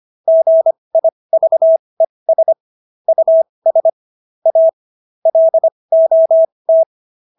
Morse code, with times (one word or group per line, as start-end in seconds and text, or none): 0.3-2.5 GIVES
3.1-3.9 US
4.5-4.7 A
5.3-6.8 LOT